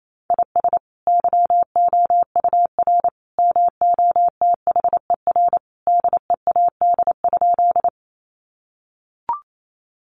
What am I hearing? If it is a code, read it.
Morse code, heard as a keyed tone, about 28 wpm, 700 Hz. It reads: IS YOUR MOTHER DEAD?
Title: Sentences from Top 1500 Words 28wpm